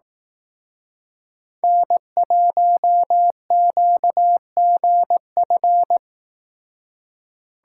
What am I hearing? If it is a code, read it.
Morse code, heard as a keyed tone, about 18 wpm, 700 Hz. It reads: N1QGF
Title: Speed Racing - USA 1x3 Call Signs 15wpm